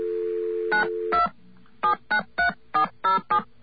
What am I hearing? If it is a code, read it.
DTMF, heard as telephone touch tones, tones of about 100 ms. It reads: 92 793407